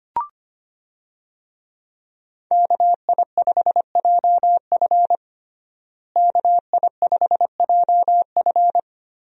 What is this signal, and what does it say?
Morse code, heard as a keyed tone, about 25 wpm, 700 Hz. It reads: KI5JF KI5JF
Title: USA Call Signs - 5 Characters - Repeated 3x - 25wpm